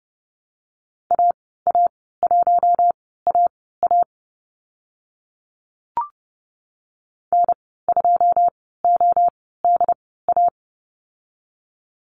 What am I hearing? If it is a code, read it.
Morse code, heard as a keyed tone, about 30 wpm, spaced as for 17 wpm, 700 Hz. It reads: AA1AA N2ODA
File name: Lesson 15g - Number 5 - Call Signs 30wpm